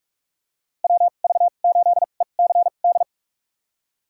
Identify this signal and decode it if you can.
Morse code, heard as a keyed tone, about 47 wpm, spaced as for 33 wpm, 700 Hz. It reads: WV8ECD
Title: USA Call Signs - 2x Character Spacing - 45wpm